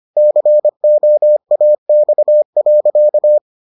Morse code, heard as a keyed tone, about 25 wpm, 600 Hz. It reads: COAX.